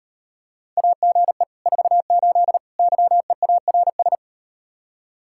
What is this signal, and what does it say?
Morse code, heard as a keyed tone, about 38 wpm, 700 Hz. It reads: AGE 48 YEARS